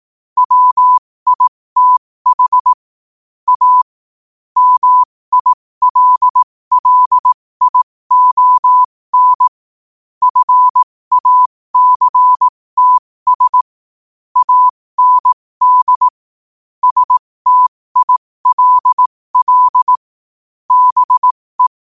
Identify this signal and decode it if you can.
Morse code, heard as a keyed tone, about 18 wpm, spaced as for 15 wpm, 1000 Hz. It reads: WITH A MILLION FACTS AND STILL BE
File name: qotdfiles / 15wpm